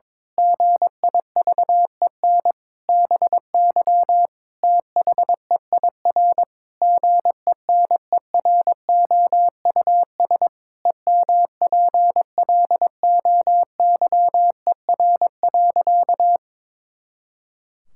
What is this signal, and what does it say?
Morse code, heard as a keyed tone, about 22 wpm, 700 Hz. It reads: GIVEN BY THEIR GENEROUS EMPLOYER.
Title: Pangram Challenge v2 - 22wpm